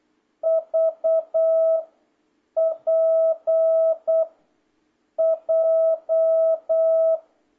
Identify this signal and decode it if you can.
Morse code, heard as a keyed tone, about 8 wpm, 650 Hz. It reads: V P J